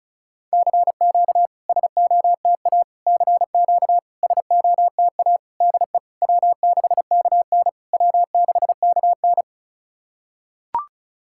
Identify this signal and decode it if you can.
Morse code, heard as a keyed tone, about 36 wpm, 690 Hz. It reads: CQ SOTA CQ SOTA DE W6KN W6KN
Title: SOTA QSO Practice 35wpm